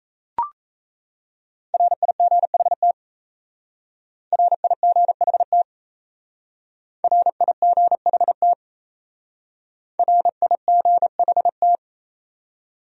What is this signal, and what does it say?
Morse code, heard as a keyed tone, about 35 wpm, 700 Hz. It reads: RIGHT RIGHT RIGHT RIGHT